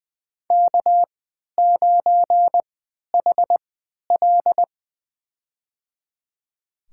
Morse code, heard as a keyed tone, about 20 wpm, spaced as for 11 wpm, 700 Hz. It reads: K9HL
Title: USA Call Signs - 4 Characters - 3x Character Spacing - 20wpm